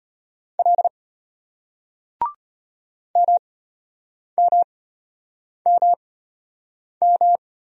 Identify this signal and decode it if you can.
Morse code, heard as a keyed tone, about 33 wpm, spaced as for 8 wpm, 700 Hz. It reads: L MMMM